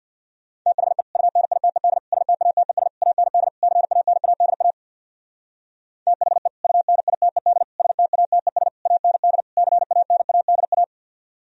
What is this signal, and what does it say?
Morse code, heard as a keyed tone, about 64 wpm, 700 Hz. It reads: THE UNITED STATES AND CANADA THE UNITED STATES AND CANADA